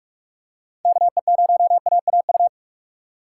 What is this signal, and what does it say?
Morse code, heard as a keyed tone, about 46 wpm, 700 Hz. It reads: KE0AAU
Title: USA Call Signs - 6 Characters - 2x Character Spacing - 45wpm